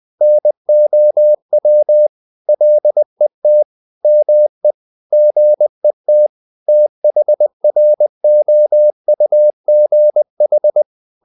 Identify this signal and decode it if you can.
Morse code, heard as a keyed tone, about 20 wpm, 600 Hz. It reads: NOW LET ME GET THROUGH
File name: the old man and the sea part 2 20wpm